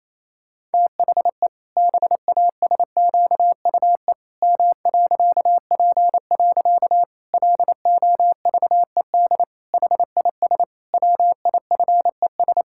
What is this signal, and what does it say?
Morse code, heard as a keyed tone, about 28 wpm, 700 Hz. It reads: THE BASQUE M.P. LOVED HIS WIFES
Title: Pangram Challenge v2 - 28wpm